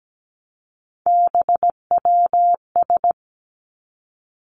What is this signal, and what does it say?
Morse code, heard as a keyed tone, about 17 wpm, 700 Hz. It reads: BWS
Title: Three Challenging Letter-Number Combinations 17wpm